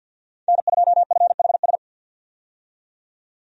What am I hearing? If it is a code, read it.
Morse code, heard as a keyed tone, about 53 wpm, 700 Hz. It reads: N2FHS
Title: USA Call Signs - 5 Characters - 50wpm